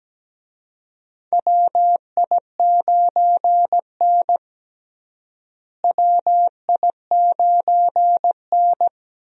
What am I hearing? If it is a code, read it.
Morse code, heard as a keyed tone, about 17 wpm, 700 Hz. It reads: WI9N WI9N
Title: USA Call Signs - 4 Characters - Repeated 3x - 17wpm